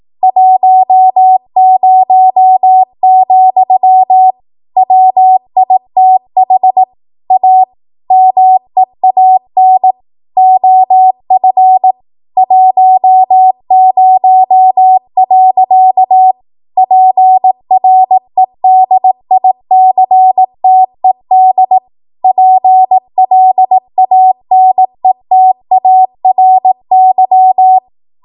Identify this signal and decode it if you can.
Morse code, heard as a keyed tone, about 18 wpm, 750 Hz. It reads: 10, WITH A MEAN OF 10. PREDICTED PLANETARY
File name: Morse / 18WPM